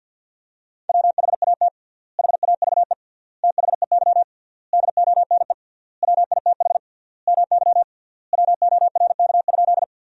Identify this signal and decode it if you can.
Morse code, heard as a keyed tone, about 53 wpm, 700 Hz. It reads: WHAT HAVE THEY DONE WITH MY WORK?
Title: 3x Word-Spacing - Sentences from Top 500 Words - 50wpm